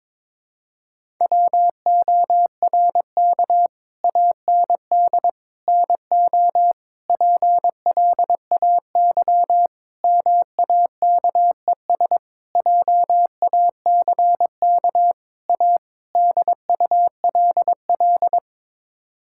Morse code, heard as a keyed tone, about 22 wpm, 700 Hz. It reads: WORK AND NO PLAY MAKES JACK A DULL
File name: practice / 22wpm / Sets of 8 Words - 22wpm